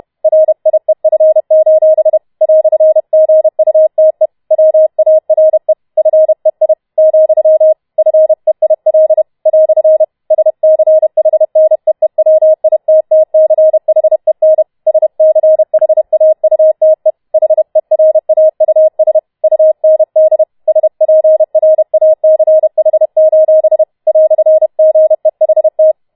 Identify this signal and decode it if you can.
Morse code, heard as a keyed tone, about 31 wpm, 630 Hz. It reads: RIEF: "GUTE WARE FEI , FEIL " SCHNEEWITTCHEN SCHAUTE HERAUS UND SPRACH: "GEHT
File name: Schneewittchen Teil4 32wpm